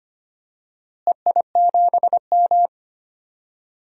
Morse code, heard as a keyed tone, about 25 wpm, 700 Hz. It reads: EI7M
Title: DXCC-100 Call Signs - 4 Characters - Repeated 3x - 25wpm